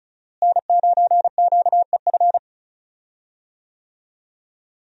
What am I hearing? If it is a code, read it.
Morse code, heard as a keyed tone, about 34 wpm, 700 Hz. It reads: N9QEF
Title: USA Call Signs - 5 Characters - Repeated 2x - 35wpm